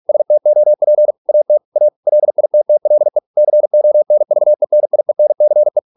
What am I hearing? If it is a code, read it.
Morse code, heard as a keyed tone, about 46 wpm, 600 Hz. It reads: STOP AT A LITTLE CONVENIENCE